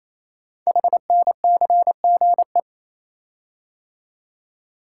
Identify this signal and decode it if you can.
Morse code, heard as a keyed tone, about 27 wpm, 700 Hz. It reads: HNCGE
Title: Code Groups - 5 Letters - 28wpm